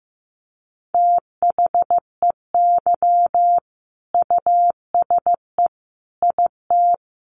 Morse code, heard as a keyed tone, about 15 wpm, 700 Hz. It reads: THEY USE IT
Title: Sentences from Top 1400 Words 15wpm